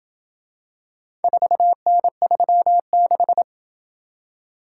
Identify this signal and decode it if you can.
Morse code, heard as a keyed tone, about 27 wpm, 700 Hz. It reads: 4N36